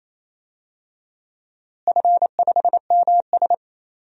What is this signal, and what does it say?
Morse code, heard as a keyed tone, about 28 wpm, 700 Hz. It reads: F5MS
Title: DXCC-100 Call Signs - 4 Characters - Repeated 3x - 28wpm